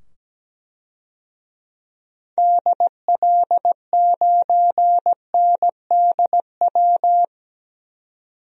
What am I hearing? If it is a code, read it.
Morse code, heard as a keyed tone, about 17 wpm, 700 Hz. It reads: DL9NDW
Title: DXCC-100 Call Signs - 6 Characters - 17wpm